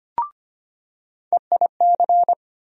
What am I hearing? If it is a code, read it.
Morse code, heard as a keyed tone, about 25 wpm, 700 Hz. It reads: EIC
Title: USA Call Sign Postfixes - 3 Letters 25wpm